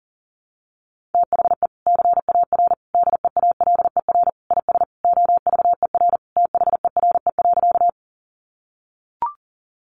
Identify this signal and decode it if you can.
Morse code, heard as a keyed tone, about 40 wpm, 700 Hz. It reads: THE CAR DEALER IS OVER THERE.
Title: Sentences from Top 1100 Words 40wpm